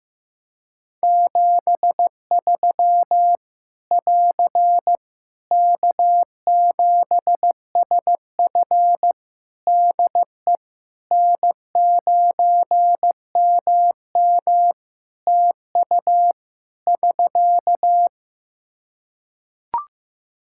Morse code, heard as a keyed tone, about 15 wpm, 700 Hz. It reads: 73 <AR> K7SF DE N9MM TU <SK>
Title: K1USN SST QSO Practice 15wpm